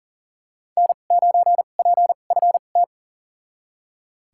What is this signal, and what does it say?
Morse code, heard as a keyed tone, about 40 wpm, spaced as for 29 wpm, 700 Hz. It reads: N9PFT
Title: USA Call Signs - 5 Characters - 2x Character Spacing - 40wpm